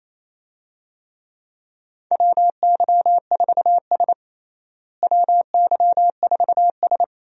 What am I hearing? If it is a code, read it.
Morse code, heard as a keyed tone, about 27 wpm, 700 Hz. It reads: WY4S WY4S
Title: USA Call Signs - 4 Characters - Repeated 2x - 28wpm